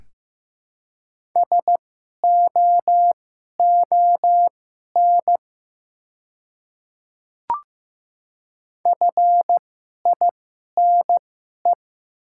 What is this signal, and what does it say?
Morse code, heard as a keyed tone, about 15 wpm, spaced as for 11 wpm, 690 Hz. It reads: SOON FINE
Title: 2x Character Spacing - Top 500 Words - 15wpm